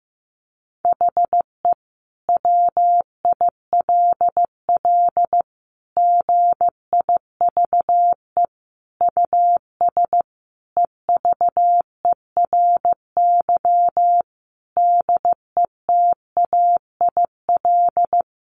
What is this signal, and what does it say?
Morse code, heard as a keyed tone, about 14 wpm, 700 Hz. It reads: HE WILL GIVE US EVERY DETAIL
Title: Sentences from Top 1500 Words 15wpm